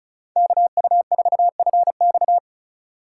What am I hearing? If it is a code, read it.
Morse code, heard as a keyed tone, about 36 wpm, 700 Hz. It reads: KU4FX
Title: USA Call Signs - 5 Characters - 3x Character Spacing - 35wpm